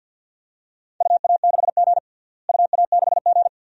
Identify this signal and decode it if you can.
Morse code, heard as a keyed tone, about 45 wpm, 700 Hz. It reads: UA6G UA6G
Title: DXCC-100 Call Signs - 4 Characters - Repeated 2x - 50wpm